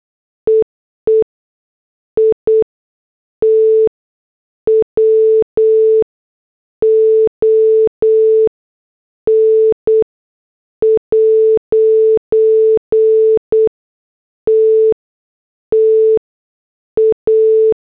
Morse code, heard as a keyed tone, about 8 wpm, spaced as for 6 wpm, 440 Hz. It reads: EE IT WON'T TA